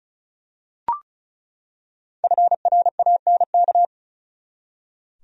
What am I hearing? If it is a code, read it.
Morse code, heard as a keyed tone, about 34 wpm, 700 Hz. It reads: FRANK
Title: USA Common Names - Max 5 Letters - 35wpm